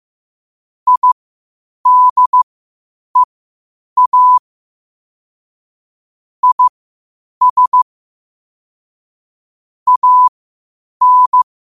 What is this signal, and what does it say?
Morse code, heard as a keyed tone, about 15 wpm, spaced as for 8 wpm, 1000 Hz. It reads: IDEA IS AN
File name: qotdfiles / 8wpm